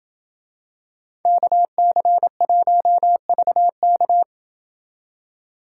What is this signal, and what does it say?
Morse code, heard as a keyed tone, about 27 wpm, 700 Hz. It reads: KC1VK